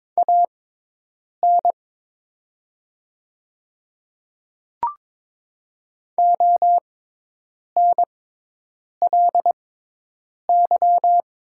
Morse code, heard as a keyed tone, about 22 wpm, spaced as for 8 wpm, 700 Hz. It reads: AN ONLY
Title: 6x Character Spacing - Top 300 Words - 22wpm